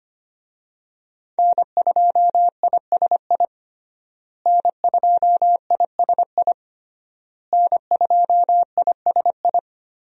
Morse code, heard as a keyed tone, about 25 wpm, 700 Hz. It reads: N2ISI N2ISI N2ISI
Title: USA Call Signs - 5 Characters - Repeated 3x - 25wpm